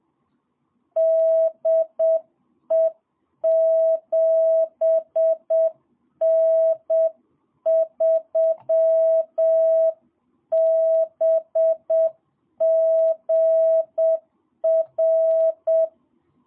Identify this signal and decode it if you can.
Morse code, heard as a keyed tone, about 7 wpm, 660 Hz. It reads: DE 7N3BGR
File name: jh8cbh / 2022cw